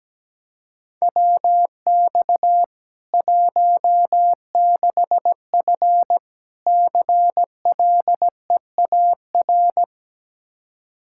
Morse code, heard as a keyed tone, about 17 wpm, 700 Hz. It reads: WX 16F CLEAR
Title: Practice QSO - Exchange 2 - Info - 17wpm